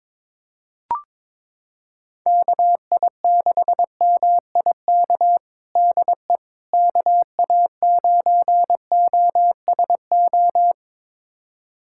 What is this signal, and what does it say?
Morse code, heard as a keyed tone, about 21 wpm, 700 Hz. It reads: KI6MIK DE KA9OSO
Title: USA Callsign DE Callsign - 6 Characters - 22wpm